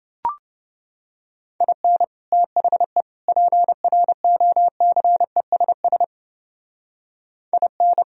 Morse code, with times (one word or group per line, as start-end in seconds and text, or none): 1.6-2.0 IN
2.3-3.0 THE
3.3-6.0 PROCESS
7.5-8.0 IN